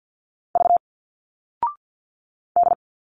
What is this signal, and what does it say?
Morse code, heard as a keyed tone, about 50 wpm, 700 Hz. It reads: V D
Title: Single Letters 50wpm